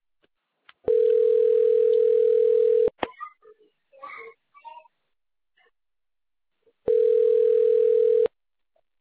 Ringback tone (North America), 440 + 480 Hz.